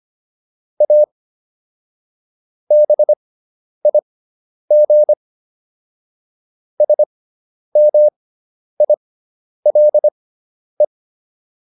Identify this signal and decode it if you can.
Morse code, heard as a keyed tone, about 25 wpm, spaced as for 10 wpm, 600 Hz. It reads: A BIG SMILE